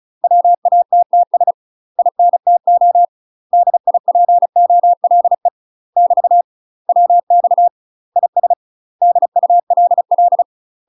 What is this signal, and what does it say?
Morse code, heard as a keyed tone, about 36 wpm, 700 Hz. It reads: WATTS INTO DIPOLE <BT> WX IS DULL
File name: UK MASTER QSOs TRAINING / 35WPM / UK MASTER QSOs 0028